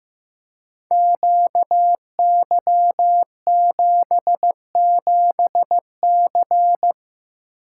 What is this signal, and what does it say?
Morse code, heard as a keyed tone, about 15 wpm, 700 Hz. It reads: QY77C
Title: Code Groups - 5 Letter:Numbers - 15wpm